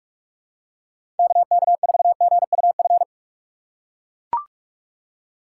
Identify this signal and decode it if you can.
Morse code, heard as a keyed tone, about 44 wpm, 700 Hz. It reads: KK4GUF
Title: USA Call Signs - 6 Characters - 2x Character Spacing - 45wpm